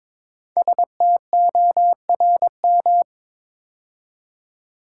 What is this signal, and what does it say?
Morse code, heard as a keyed tone, about 22 wpm, 700 Hz. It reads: STORM
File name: 5 Letter Words - First 3 Letters Given - 22wpm